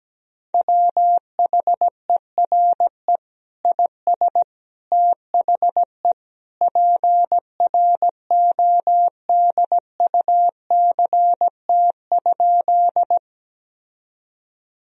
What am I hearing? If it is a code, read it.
Morse code, heard as a keyed tone, about 17 wpm, 700 Hz. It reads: WHERE IS THE PRODUCT?
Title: Sentences from Top 1000 Words - Review Part 2 - 17wpm